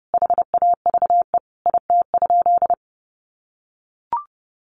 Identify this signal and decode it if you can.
Morse code, heard as a keyed tone, about 29 wpm, 700 Hz. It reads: HAVE IT?
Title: Sentences from Top 100 Words 30wpm